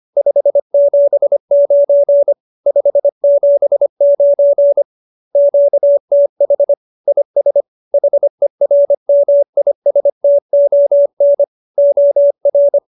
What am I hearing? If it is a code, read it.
Morse code, heard as a keyed tone, about 25 wpm, 570 Hz. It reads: 579 579 QTH IS HERMISTON OR